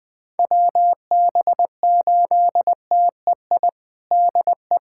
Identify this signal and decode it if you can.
Morse code, heard as a keyed tone, about 20 wpm, 700 Hz. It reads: WB8TEI DE
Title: USA Callsign DE Callsign 20wpm